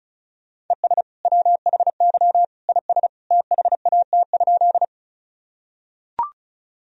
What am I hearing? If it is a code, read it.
Morse code, heard as a keyed tone, about 35 wpm, 700 Hz. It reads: ES WHY IS THAT?
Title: Sentences from Top 200 Words and Abbreviations 35wpm